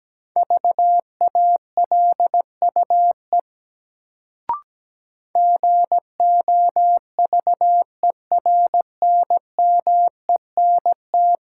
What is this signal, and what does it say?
Morse code, heard as a keyed tone, about 17 wpm, 700 Hz. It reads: VALUE GOVERNMENT
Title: Sentences from Top 600 Words - New Word Drill - 17wpm